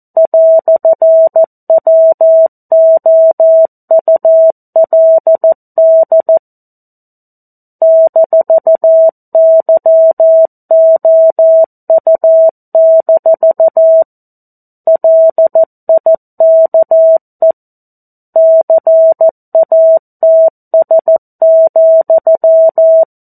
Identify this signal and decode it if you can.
Morse code, heard as a keyed tone, about 14 wpm, 650 Hz. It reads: "WOULD -YOU- LIKE CATS,
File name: 02-Chapter-02 14 WPM